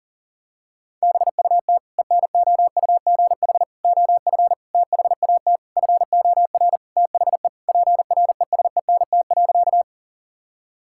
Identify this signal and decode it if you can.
Morse code, heard as a keyed tone, about 40 wpm, 700 Hz. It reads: BUT ENOUGH OF THAT FOR THE PRESENT.
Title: Sentences from Top 1000 Words - Review Part 5 - 40wpm